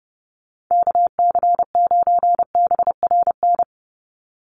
Morse code, heard as a keyed tone, about 30 wpm, 700 Hz. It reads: KC9BRN
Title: USA Call Signs 30wpm